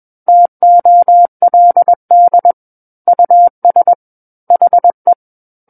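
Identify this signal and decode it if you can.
Morse code, heard as a keyed tone, about 21 wpm, 700 Hz. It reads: TOLD US HE
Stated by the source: Morse code practice reading of a book chapter at 21 words per minute.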